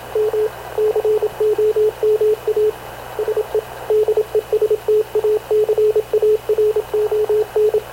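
Morse code, heard as a keyed tone, about 27 wpm, 430 Hz, with radio noise. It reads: M COMA SE DESTACARON
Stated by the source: Havana Radio / CLA sending Spanish press (PX) in Morse, recorded by Radio Officer Paul Dumesnil onboard CNAV QUEST/CZDO.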